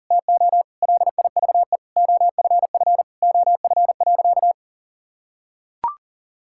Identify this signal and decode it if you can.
Morse code, heard as a keyed tone, about 40 wpm, 700 Hz. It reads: TO LIVE OFF OF.
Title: Sentences from Top 1000 Words - Review Part 5 - 40wpm